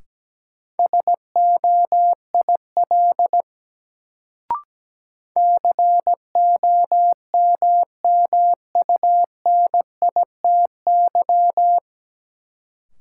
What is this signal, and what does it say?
Morse code, heard as a keyed tone, about 17 wpm, 700 Hz. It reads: SOIL COMMUNITY
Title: Sentences from Top 1000 Words - Review Part 4 - New Word Drill - 17wpm